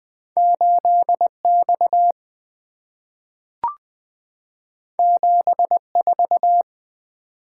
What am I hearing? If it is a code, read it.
Morse code, heard as a keyed tone, about 20 wpm, 700 Hz. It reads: 8X 74
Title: Two Letter-Number Combinations - 20wpm